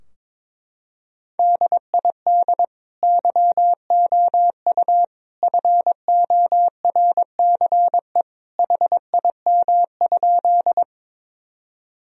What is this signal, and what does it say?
Morse code, heard as a keyed tone, about 21 wpm, 700 Hz. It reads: DID YOU FORCE HIM?
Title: Sentences from Top 1000 Words - Review Part 3 - 22wpm